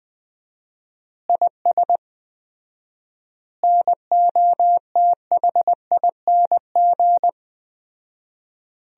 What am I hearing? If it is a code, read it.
Morse code, heard as a keyed tone, about 20 wpm, 700 Hz. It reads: IS NOTHING
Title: 4x Word-Spacing - Sentences from Top 500 Words - 20wpm